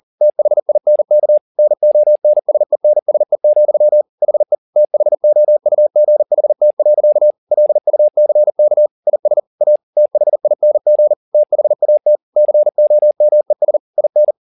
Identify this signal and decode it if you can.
Morse code, heard as a keyed tone, about 40 wpm, 600 Hz. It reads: THINK NONSENSE, HE THOUGHT. LUCK IS A THING THAT COMES IN
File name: the old man and the sea 2 40wpm